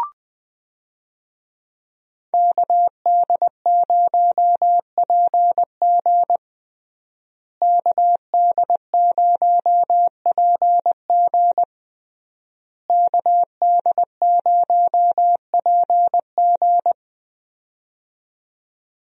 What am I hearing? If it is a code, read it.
Morse code, heard as a keyed tone, about 20 wpm, 700 Hz. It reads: KD0PG KD0PG KD0PG
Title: USA Call Signs - 5 Characters - Repeated 3x - 20wpm